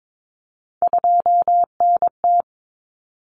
Morse code, heard as a keyed tone, about 22 wpm, 700 Hz. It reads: 2NT
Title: Common Cut Numbers - 3-Digit 22wpm